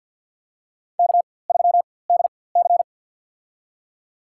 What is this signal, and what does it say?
Morse code, heard as a keyed tone, about 52 wpm, spaced as for 23 wpm, 700 Hz. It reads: K3DC
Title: USA Call Signs - 4x Character Spacing - 50wpm